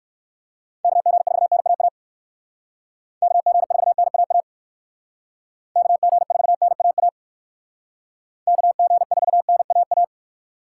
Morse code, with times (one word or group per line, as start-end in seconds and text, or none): 0.8-1.9 KG4NAA
3.2-4.4 KG4NAA
5.8-7.1 KG4NAA
8.5-10.0 KG4NAA